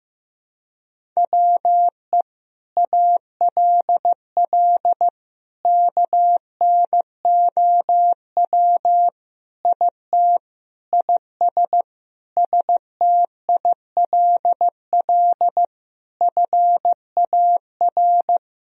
Morse code, heard as a keyed tone, about 15 wpm, 700 Hz. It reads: WE ALL KNOW IT IS STILL FAR
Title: Sentences from Top 1000 Words - Review Part 1 - 15wpm